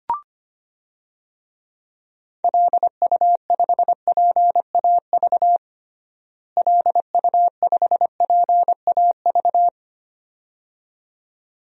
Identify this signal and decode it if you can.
Morse code, heard as a keyed tone, about 25 wpm, 700 Hz. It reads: LU5PAV LU5PAV
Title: DXCC-100 Call Signs - 6 Characters - Repeated 2x - 25wpm